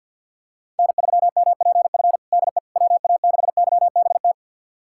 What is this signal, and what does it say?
Morse code, heard as a keyed tone, about 51 wpm, 690 Hz. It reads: N3MPF DE WA6YBT
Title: USA Callsign DE Callsign 50wpm